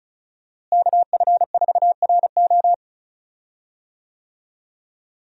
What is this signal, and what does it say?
Morse code, heard as a keyed tone, about 34 wpm, 700 Hz. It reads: KF4RO
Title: USA Call Signs - 5 Characters - Repeated 2x - 35wpm